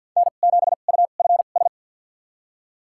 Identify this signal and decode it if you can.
Morse code, heard as a keyed tone, about 50 wpm, spaced as for 36 wpm, 700 Hz. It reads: N7UFS
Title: USA Call Signs - 5 Characters - 2x Character Spacing - 50wpm